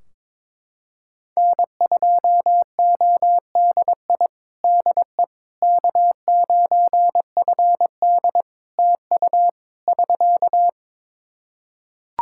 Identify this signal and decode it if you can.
Morse code, heard as a keyed tone, about 22 wpm, 700 Hz. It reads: N2ODI DE K9FD TU <SK>
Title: Practice QSO - Exchange 4 - Ending - 22wpm